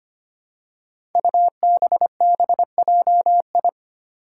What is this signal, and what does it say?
Morse code, heard as a keyed tone, about 25 wpm, 700 Hz. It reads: UBBJI
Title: Five Challenging Letter-Number Combinations 25wpm